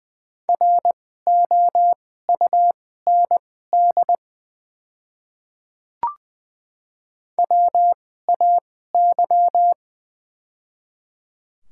Morse code, heard as a keyed tone, about 20 wpm, spaced as for 14 wpm, 700 Hz. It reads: ROUND WAY
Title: 2x Character Spacing - Top 200 Words - 20wpm